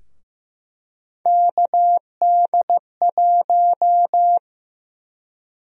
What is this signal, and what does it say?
Morse code, heard as a keyed tone, about 15 wpm, 700 Hz. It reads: KD1
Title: USA Call Sign Prefixes - 15wpm